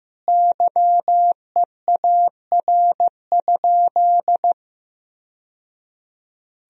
Morse code, heard as a keyed tone, about 16 wpm, 700 Hz. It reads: YEAR?
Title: Sentences from Top 1000 Words - Review Part 2 - 15wpm